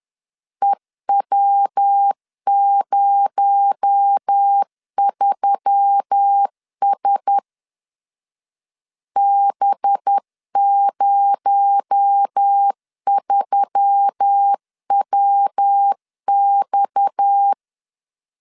Morse code, heard as a keyed tone, about 10 wpm, 800 Hz. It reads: EW03S B03WX
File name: LEZIONI / 6 PKX0 / 6 PKX0 10wpm 2707